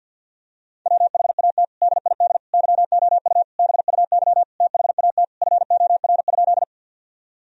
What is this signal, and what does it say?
Morse code, heard as a keyed tone, about 51 wpm, 700 Hz. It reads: WHAT DID YOU BUY THAT FOR?
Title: Sentences from Top 1000 Words - Review Part 4 - 50wpm